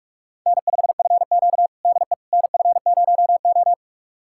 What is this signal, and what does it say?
Morse code, heard as a keyed tone, about 44 wpm, 700 Hz. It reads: N5FQ DE NF0O
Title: USA Callsign DE Callsign - 4 Characters - 45wpm